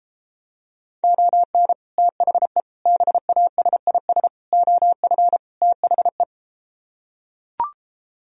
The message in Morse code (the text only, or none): ON THE BASIS OF THE